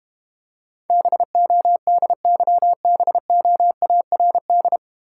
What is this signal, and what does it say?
Morse code, heard as a keyed tone, about 32 wpm, 700 Hz. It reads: BODYBOARD